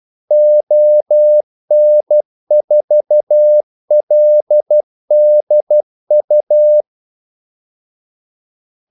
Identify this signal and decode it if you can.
Morse code, heard as a keyed tone, about 12 wpm, 600 Hz. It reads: ON4LDU
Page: 2 ekstra ordmellemrum mellem kaldesignalerne